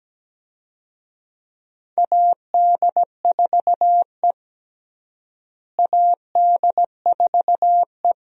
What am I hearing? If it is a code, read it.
Morse code, heard as a keyed tone, about 17 wpm, 700 Hz. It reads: AD4E AD4E
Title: USA Call Signs - 4 Characters - Repeated 2x - 17wpm